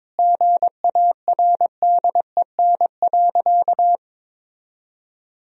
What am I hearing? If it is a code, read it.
Morse code, heard as a keyed tone, about 22 wpm, 700 Hz. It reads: GARDEN.
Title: Sentences from Top 1000 Words - Review Part 4 - 22wpm